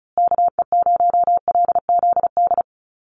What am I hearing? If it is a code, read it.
Morse code, heard as a keyed tone, about 34 wpm, 700 Hz. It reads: KE0LZD
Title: USA Call Signs Repeated 2x 35wpm